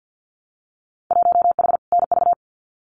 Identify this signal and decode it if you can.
Morse code, heard as a keyed tone, about 50 wpm, 700 Hz. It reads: 1H NV
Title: Winter Field Day Exchanges 50wpm